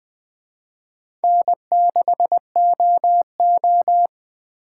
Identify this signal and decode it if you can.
Morse code, heard as a keyed tone, about 20 wpm, 700 Hz. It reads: N6OO
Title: Speed Racing - USA 1x2 Call Signs 17wpm